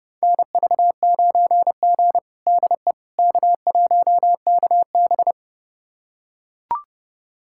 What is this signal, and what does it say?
Morse code, heard as a keyed tone, about 31 wpm, 700 Hz. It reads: NV9G DE K1KB
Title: USA Callsign DE Callsign - 4 Characters - 30wpm